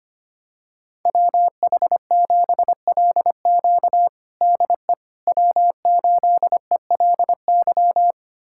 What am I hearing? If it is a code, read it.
Morse code, heard as a keyed tone, about 25 wpm, 700 Hz. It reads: WH7LQ DE W8ELY
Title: USA Callsign DE Callsign - 5 Characters - 25wpm